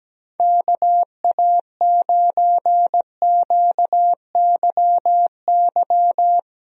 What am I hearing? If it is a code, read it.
Morse code, heard as a keyed tone, about 17 wpm, 700 Hz. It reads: KA9QYY
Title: USA Call Signs - 6 Characters - Repeated 2x - 17wpm